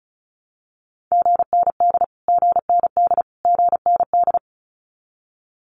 Morse code, heard as a keyed tone, about 34 wpm, 700 Hz. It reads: GND GND GND
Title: Abbreviations Repeated 3x - 35wpm